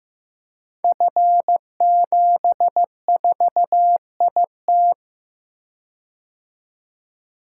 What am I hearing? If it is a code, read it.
Morse code, heard as a keyed tone, about 15 wpm, 700 Hz. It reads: F74IT
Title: Code Groups - 5 Letter:Numbers - 15wpm